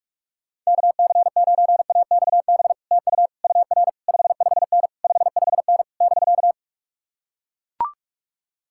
Morse code, heard as a keyed tone, about 44 wpm, 700 Hz. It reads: KC9AXB TU UR 55N 55N <BK>